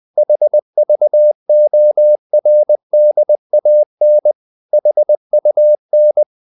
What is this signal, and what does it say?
Morse code, heard as a keyed tone, about 20 wpm, 600 Hz. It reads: HVORDAN HUN
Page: Ferie 20 wpm